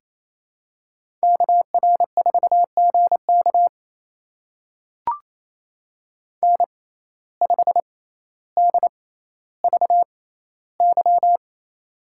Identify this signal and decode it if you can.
Morse code, heard as a keyed tone, about 28 wpm, spaced as for 14 wpm, 700 Hz. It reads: KR4GK N5DVY
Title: USA Call Signs - 5 Characters - 6x Character Spacing - 28wpm